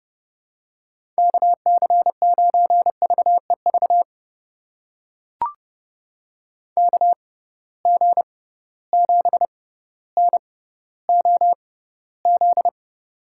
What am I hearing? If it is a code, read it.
Morse code, heard as a keyed tone, about 32 wpm, spaced as for 16 wpm, 700 Hz. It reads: KC9VEV KG7NOZ
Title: USA Call Signs - 6 Characters - 6x Character Spacing - 30wpm